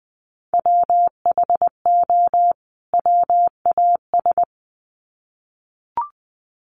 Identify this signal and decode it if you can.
Morse code, heard as a keyed tone, about 20 wpm, 700 Hz. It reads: WHO WAS
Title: Sets of 2 Words 20wpm